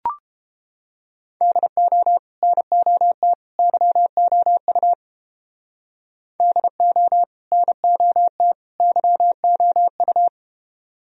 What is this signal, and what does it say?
Morse code, heard as a keyed tone, about 30 wpm, 700 Hz. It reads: DO NOT YOU DO NOT YOU